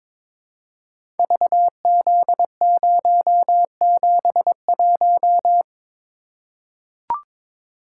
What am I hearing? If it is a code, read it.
Morse code, heard as a keyed tone, about 22 wpm, 700 Hz. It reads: VZ071
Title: Code Groups - 5 Letter:Numbers - 22wpm